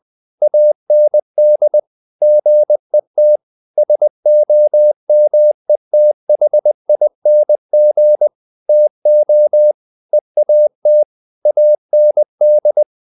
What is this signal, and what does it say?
Morse code, heard as a keyed tone, about 20 wpm, 600 Hz. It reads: AND GET SOMETHING TO EAT AND
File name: the old man and the sea part 2 20wpm